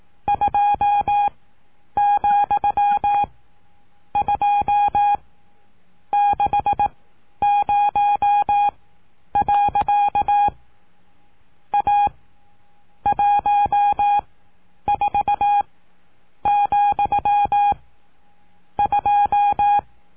Morse code, heard as a keyed tone, about 18 wpm, spaced as for 8 wpm, 820 Hz. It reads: 2,260. A 14,2
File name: cursos / CURSO DE TELEGRAFIA / Telegrafia - Trilha 19